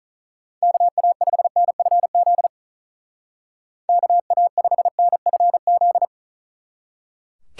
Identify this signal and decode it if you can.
Morse code, heard as a keyed tone, about 43 wpm, 700 Hz. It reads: KA5NFZ KA5NFZ
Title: Speed Racing - USA 2x3 Call Signs 35wpm